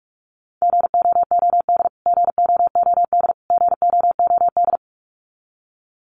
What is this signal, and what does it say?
Morse code, heard as a keyed tone, about 44 wpm, 700 Hz. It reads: GOOD GOOD GOOD
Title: Top 500 Words Repeated 45wpm